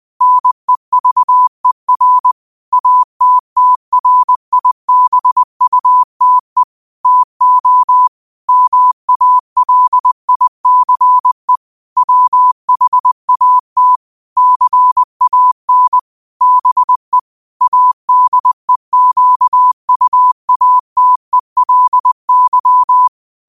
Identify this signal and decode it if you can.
Morse code, heard as a keyed tone, about 20 wpm, 1000 Hz. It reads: NEVER ATTRIBUTE TO MALICE WHAT CAN BE ADEQUATELY
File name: qotdfiles / 20wpm